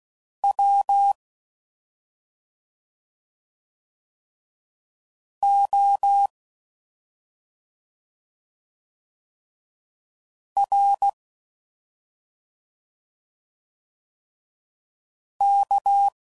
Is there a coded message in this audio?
Yes: Morse code from 0.4 to 16.1 s, heard as a keyed tone, about 16 wpm, 800 Hz. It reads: W O R K